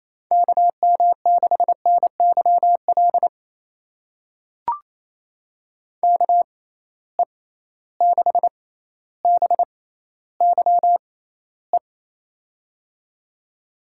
Morse code, heard as a keyed tone, about 28 wpm, spaced as for 14 wpm, 700 Hz. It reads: KM6NYL KE6BYE
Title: USA Call Signs - 6 Characters - 6x Character Spacing - 28wpm